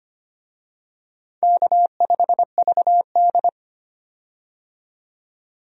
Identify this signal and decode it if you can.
Morse code, heard as a keyed tone, about 25 wpm, 700 Hz. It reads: K5VD
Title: USA Call Signs - 4 Characters - 25wpm